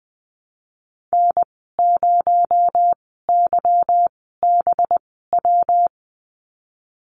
Morse code, heard as a keyed tone, about 20 wpm, spaced as for 15 wpm, 700 Hz. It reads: N0YBW
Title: 219 Lesson 35g - Number 0 - Call Signs 20wpm